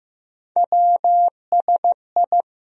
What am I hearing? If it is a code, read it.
Morse code, heard as a keyed tone, about 15 wpm, 700 Hz. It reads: WSI
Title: USA Call Sign Postfixes - 3 Letters 15wpm